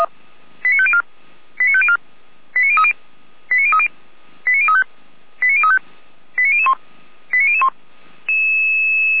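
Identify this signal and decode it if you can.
Five-tone selective call (ZVEI), heard as quick runs of five tones, tones of about 70 ms. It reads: 89504 89504 89030 89030 89035 89035 89001 89001
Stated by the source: SDR-received ZVEI five-tone call 89010, fire brigade Orlamünde.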